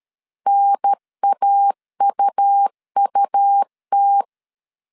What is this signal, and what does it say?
Morse code, heard as a keyed tone, about 13 wpm, 800 Hz. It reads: NAUUT
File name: CW / DET / 1 3 1500C S C 12wpm 2750